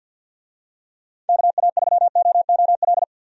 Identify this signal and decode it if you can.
Morse code, heard as a keyed tone, about 46 wpm, 700 Hz. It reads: KA3OOL
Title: USA Call Signs - 6 Characters - Repeated 3x - 50wpm